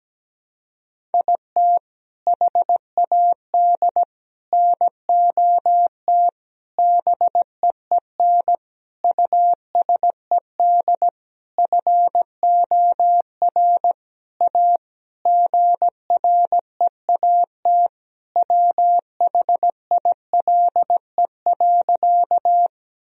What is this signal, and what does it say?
Morse code, heard as a keyed tone, about 17 wpm, 700 Hz. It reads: IT HAD NOT BEEN USED FOR A GREAT WHILE.
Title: Sentences from Top 1000 Words - Review Part 1 - 17wpm